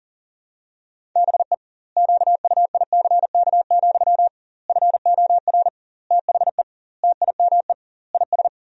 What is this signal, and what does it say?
Morse code, heard as a keyed tone, about 40 wpm, 700 Hz. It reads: BE QUICK, FOR THE TIME IS
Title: 2x Word-Spacing - Sentences from Top 500 Words - 40wpm